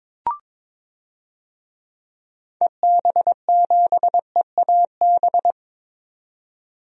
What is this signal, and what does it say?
Morse code, heard as a keyed tone, about 22 wpm, 700 Hz. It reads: EB7EAB
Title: DXCC-100 Call Signs - 6 Characters - 22wpm